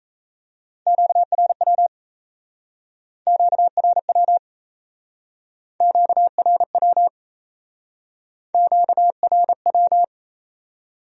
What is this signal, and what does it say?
Morse code, heard as a keyed tone, about 35 wpm, 700 Hz. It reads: QRW QRW QRW QRW
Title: Speed-Racing Q-Codes - 28wpm